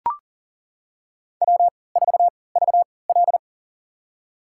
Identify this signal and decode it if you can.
Morse code, heard as a keyed tone, about 40 wpm, spaced as for 23 wpm, 700 Hz. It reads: W4VL
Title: USA Call Signs - 4 Characters - 3x Character Spacing - 40wpm